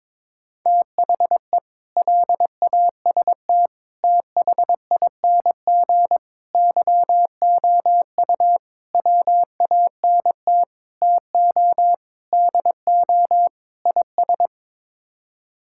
Morse code, heard as a keyed tone, about 22 wpm, 700 Hz. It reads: THE LAST THING YOU WANT TO DO IS
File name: practice / 22wpm / Sets of 8 Words - 22wpm